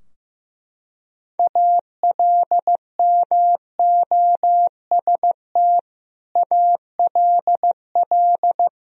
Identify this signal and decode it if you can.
Morse code, heard as a keyed tone, about 16 wpm, 700 Hz. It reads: ALMOST ALL
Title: Sentences from Top 1000 Words - Review Part 4 - 15wpm